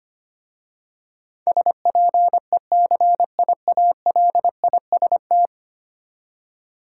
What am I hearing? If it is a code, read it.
Morse code, heard as a keyed tone, about 25 wpm, 700 Hz. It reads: SPECIALIST
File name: 10 Letter Words - First 3 Letters Given - 25wpm